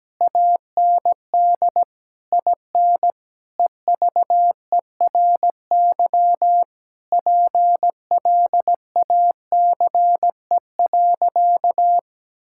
Morse code, heard as a keyed tone, about 17 wpm, 700 Hz. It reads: AND IN EVERY PLACE.
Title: Sentences from Top 1600 Words 17wpm